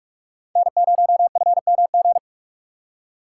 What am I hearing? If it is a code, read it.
Morse code, heard as a keyed tone, about 41 wpm, 700 Hz. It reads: N0FMG